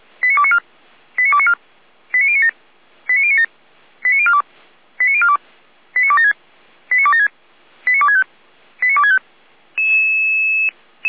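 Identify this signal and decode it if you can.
Five-tone selective call (ZVEI), heard as quick runs of five tones, tones of about 70 ms. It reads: 89294 89294 89097 89097 89042 89042 89276 89276 89265 89265
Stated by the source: ZVEI five-tone selective call 89140, SDR recording, Feuerwehr Orlamünde.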